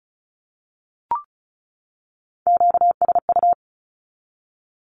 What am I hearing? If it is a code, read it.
Morse code, heard as a keyed tone, about 34 wpm, 700 Hz. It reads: QSU
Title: Q-Codes 35wpm